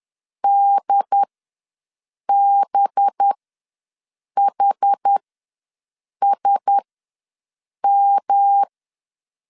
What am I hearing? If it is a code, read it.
Morse code, heard as a keyed tone, about 10 wpm, spaced as for 6 wpm, 800 Hz. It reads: DBHSM
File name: LEZIONI / 5 LBVC / 5 LB ws 10wpm 3616